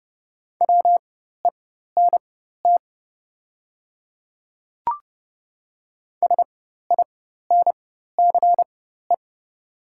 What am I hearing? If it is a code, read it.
Morse code, heard as a keyed tone, about 30 wpm, spaced as for 14 wpm, 700 Hz. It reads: WENT SINCE
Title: 4x Character Spacing - Top 300 Words - 30wpm